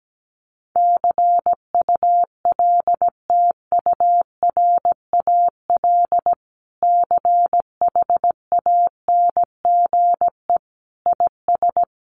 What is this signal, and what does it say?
Morse code, heard as a keyed tone, about 17 wpm, 700 Hz. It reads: CULTURAL CHANGE IS
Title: Sentences from Top 1500 Words 17wpm